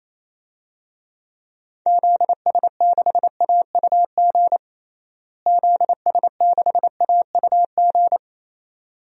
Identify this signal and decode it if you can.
Morse code, heard as a keyed tone, about 28 wpm, 700 Hz. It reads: ZS6AUG ZS6AUG
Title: DXCC-100 Call Signs - 6 Characters - Repeated 2x - 28wpm